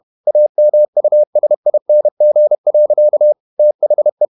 Morse code, heard as a keyed tone, about 31 wpm, 600 Hz. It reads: AMUSING. THE
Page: Never 31wpm | CW med Gnister